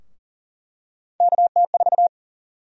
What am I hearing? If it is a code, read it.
Morse code, heard as a keyed tone, about 40 wpm, 700 Hz. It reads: KT4
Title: USA Call Sign Prefixes - Repeated 2x - 40wpm